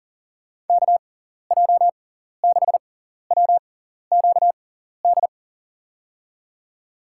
Morse code, heard as a keyed tone, about 39 wpm, spaced as for 14 wpm, 700 Hz. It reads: KJ6WQD
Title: USA Call Signs - 6x Character Spacing - 40wpm